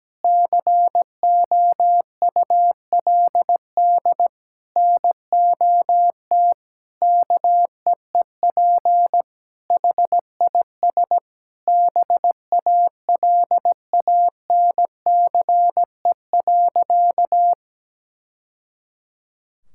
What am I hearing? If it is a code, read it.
Morse code, heard as a keyed tone, about 17 wpm, 700 Hz. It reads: COULD NOT KEEP HIS BALANCE.
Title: Sentences from Top 1000 Words - Review Part 6 - 17wpm